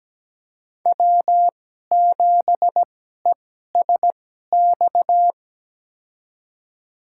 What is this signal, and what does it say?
Morse code, heard as a keyed tone, about 17 wpm, spaced as for 12 wpm, 700 Hz. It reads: W7ESX
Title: USA Call Signs - 5 Characters - 2x Character Spacing - 17wpm